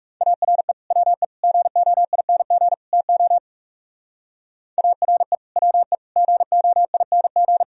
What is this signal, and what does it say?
Morse code, heard as a keyed tone, about 39 wpm, 700 Hz. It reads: ARE WE GOING TO ARE WE GOING